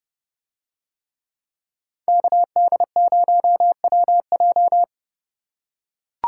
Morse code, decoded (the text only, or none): KD0WJ